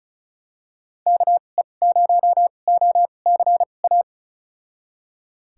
Morse code, heard as a keyed tone, about 34 wpm, spaced as for 25 wpm, 700 Hz. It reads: KE0OCA
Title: USA Call Signs - 6 Characters - 2x Character Spacing - 35wpm